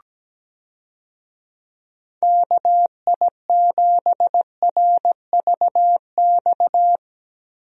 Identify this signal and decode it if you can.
Morse code, heard as a keyed tone, about 17 wpm, 700 Hz. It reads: KI7RVX
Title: USA Call Signs - 6 Characters - Repeated 2x - 17wpm